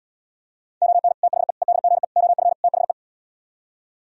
Morse code, heard as a keyed tone, about 77 wpm, 700 Hz. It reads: ON THE EDGE OF THE